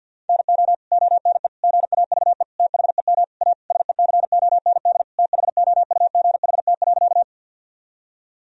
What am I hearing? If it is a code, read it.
Morse code, heard as a keyed tone, about 48 wpm, 700 Hz. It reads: NO ONE GAVE THEM A SECOND THOUGHT.
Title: Sentences from Top 1000 Words - Review Part 1 - 50wpm